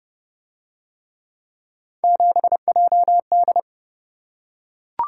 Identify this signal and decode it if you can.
Morse code, heard as a keyed tone, about 27 wpm, 700 Hz. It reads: 7JD